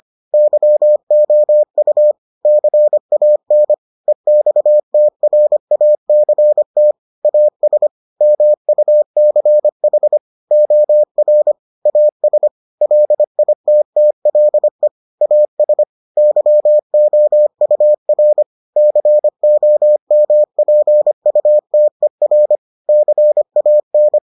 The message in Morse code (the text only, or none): YOU CAN EXTRACT AS MUCH OR AS LITTLE AS YOUR COMPUTER CAN